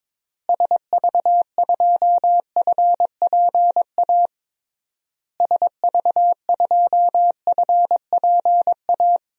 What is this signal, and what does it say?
Morse code, heard as a keyed tone, about 22 wpm, 700 Hz. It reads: SV2FPA SV2FPA
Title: DXCC-100 Call Signs - 6 Characters - Repeated 3x - 22wpm